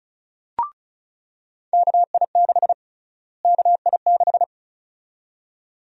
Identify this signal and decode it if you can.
Morse code, heard as a keyed tone, about 34 wpm, 700 Hz. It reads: KI6 KI6
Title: USA Call Sign Prefixes - Repeated 2x - 35wpm